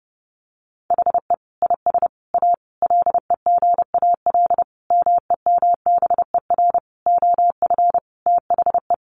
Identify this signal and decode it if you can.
Morse code, heard as a keyed tone, about 30 wpm, 700 Hz. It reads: HE IS A LEGAL MEMBER OF THE
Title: Sentences from Top 1200 Words 30wpm